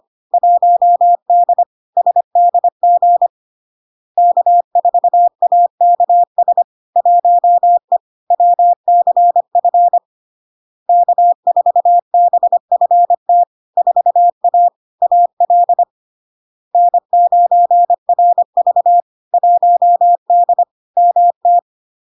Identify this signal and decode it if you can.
Morse code, heard as a keyed tone, about 25 wpm, 700 Hz. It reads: 1D SDG K4AKS 1E WCF K4BFT 4A AL N9RV 1D MT
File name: LARK-FD-2022-Log-in-CW-25wpm-700-Hz